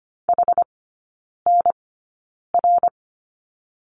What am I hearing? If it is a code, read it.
Morse code, heard as a keyed tone, about 25 wpm, spaced as for 9 wpm, 700 Hz. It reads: HNR
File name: Lesson 15e - Number 5 - Three Characters 25wpm